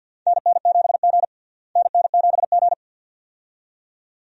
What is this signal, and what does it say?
Morse code, heard as a keyed tone, about 52 wpm, 700 Hz. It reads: NN7G NN7G
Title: USA Call Signs - 4 Characters - Repeated 2x - 50wpm